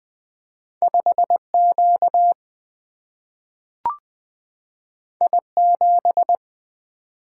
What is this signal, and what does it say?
Morse code, heard as a keyed tone, about 20 wpm, 700 Hz. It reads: 5 Q I 7
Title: Single-Letter Single-Number 20wpm